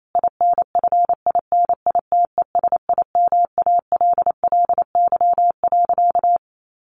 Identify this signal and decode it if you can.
Morse code, heard as a keyed tone, about 28 wpm, 700 Hz. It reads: INFINITESIMALLY.
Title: A Sound Of Thunder 28wpm